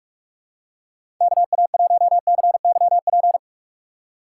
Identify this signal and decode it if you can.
Morse code, heard as a keyed tone, about 46 wpm, 700 Hz. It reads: KA1CYP
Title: USA Call Signs - 6 Characters - 3x Character Spacing - 45wpm